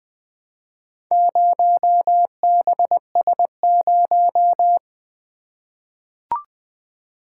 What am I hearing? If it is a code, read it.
Morse code, heard as a keyed tone, about 20 wpm, 700 Hz. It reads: 0BS0
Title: Four Letter-Number Combinations - 20wpm